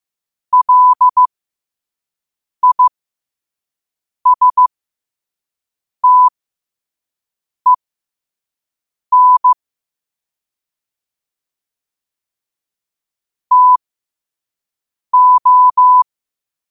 Morse code, heard as a keyed tone, about 15 wpm, spaced as for 5 wpm, 1000 Hz. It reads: LISTEN TO
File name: qotdfiles / 5wpm